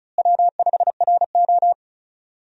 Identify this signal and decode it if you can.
Morse code, heard as a keyed tone, about 36 wpm, 700 Hz. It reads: W5RO
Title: USA Call Signs - 4 Characters - Repeated 2x - 35wpm